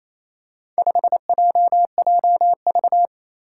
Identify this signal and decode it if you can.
Morse code, heard as a keyed tone, about 27 wpm, 700 Hz. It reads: 5JJV
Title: Four Challenging Letter-Number Combinations 28wpm